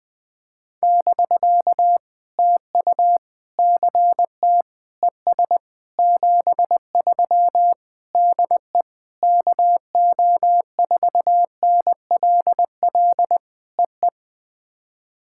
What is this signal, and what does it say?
Morse code, heard as a keyed tone, about 20 wpm, 700 Hz. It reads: <BK> TU CT ES 73 DE KO4NLL EE
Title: POTA QSO Practice - 20wpm